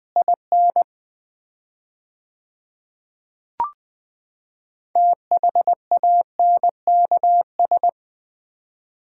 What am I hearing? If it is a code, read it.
Morse code, heard as a keyed tone, about 20 wpm, 700 Hz. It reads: IN THANKS
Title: 4x Word-Spacing - Sets of 5 Words - 20wpm